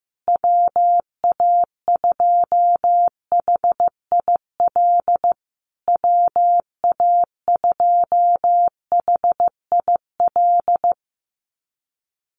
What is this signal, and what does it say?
Morse code, heard as a keyed tone, about 15 wpm, 700 Hz. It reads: WA2HIL WA2HIL
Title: USA Call Signs Repeated 3x 15wpm